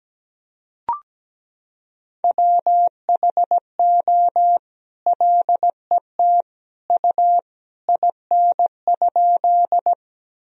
Morse code, heard as a keyed tone, about 17 wpm, 700 Hz. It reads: WHO LET U IN?
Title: Sentences from Top 300 Words and Abbreviations 17wpm